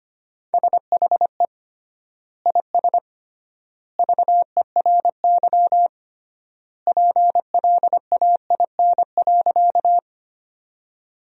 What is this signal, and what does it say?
Morse code, heard as a keyed tone, about 25 wpm, 700 Hz. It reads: SHE IS VERY PLAIN.
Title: 3x Word-Spacing - Sentences from Top 400 Words - 25wpm